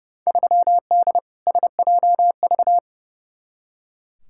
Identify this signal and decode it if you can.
Morse code, heard as a keyed tone, about 32 wpm, 700 Hz. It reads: 3D SJV
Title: Field Day Exchanges - 30wpm